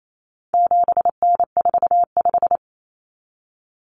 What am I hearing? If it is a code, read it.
Morse code, heard as a keyed tone, about 28 wpm, 700 Hz. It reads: 7N45
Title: Common Cut Numbers - 4-Digit 28wpm